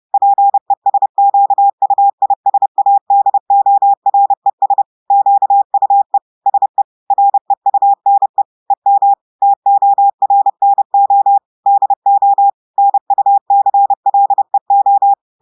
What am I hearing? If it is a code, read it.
Morse code, heard as a keyed tone, about 30 wpm, 800 Hz. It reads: PESQUISADORES QUE SE REUNE EM TORNO DO NUCLEO
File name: audios / portugais / portugais texte en clair 1 vitesse 30 mots minutes